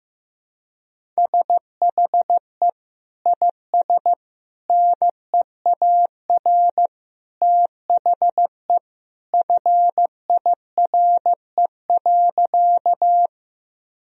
Morse code, heard as a keyed tone, about 15 wpm, 700 Hz. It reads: SHE IS NEAR THE FIRE.
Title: Sentences from Top 1000 Words - Review Part 1 - 15wpm